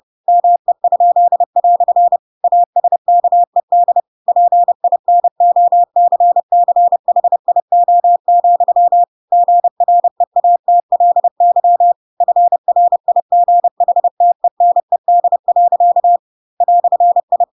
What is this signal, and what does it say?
Morse code, heard as a keyed tone, about 30 wpm, 700 Hz. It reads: ME?" ASKED PINOCCHIO, GREATLY FRIGHTENED. "I
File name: Pinocchio / Pinocchio ch4 30 30 ik1hge